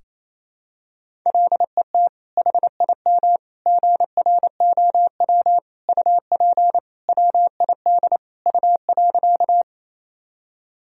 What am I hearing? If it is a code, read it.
Morse code, heard as a keyed tone, about 28 wpm, 700 Hz. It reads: LET HIM GROW UP WID U.
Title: Sentences from Top 300 Words and Abbreviations 28wpm